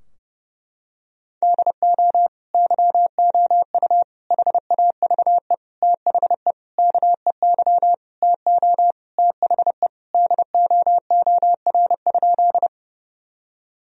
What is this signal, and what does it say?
Morse code, heard as a keyed tone, about 30 wpm, 700 Hz. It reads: DO YOU HAVE THE KEY TO THE DOOR?
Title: Sentences from Top 1000 Words - Review Part 4 - 30wpm